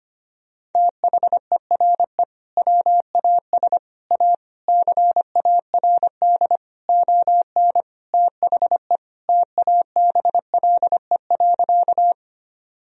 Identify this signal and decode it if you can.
Morse code, heard as a keyed tone, about 25 wpm, 700 Hz. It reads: THERE WAS A CARD ON THE TABLE.
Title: Sentences from Top 1000 Words - Review Part 4 - 25wpm